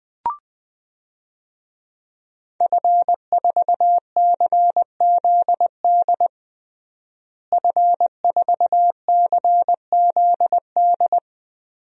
Morse code, heard as a keyed tone, about 20 wpm, 700 Hz. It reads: F4CZD F4CZD
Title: DXCC-100 Call Signs - 5 Characters - Repeated 2x - 20wpm